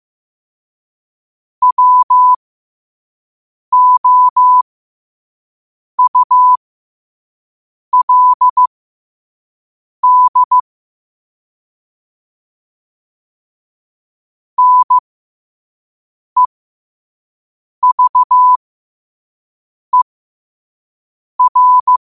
Morse code, heard as a keyed tone, about 15 wpm, spaced as for 5 wpm, 1000 Hz. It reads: WOULD NEVER
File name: qotdfiles / 5wpm